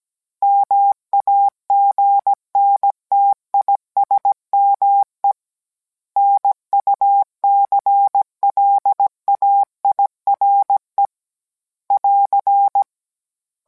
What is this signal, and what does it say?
Morse code, heard as a keyed tone, about 17 wpm, 800 Hz. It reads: MAGNTISME NUCLAIRE <AR>